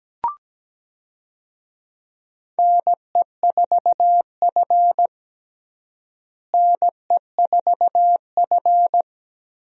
Morse code, heard as a keyed tone, about 17 wpm, 700 Hz. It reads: NE4F NE4F
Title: USA Call Signs - 4 Characters - Repeated 2x - 17wpm